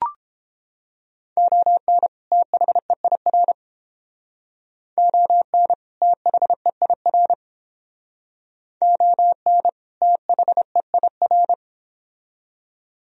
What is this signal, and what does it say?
Morse code, heard as a keyed tone, about 31 wpm, 700 Hz. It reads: ON THEIR ON THEIR ON THEIR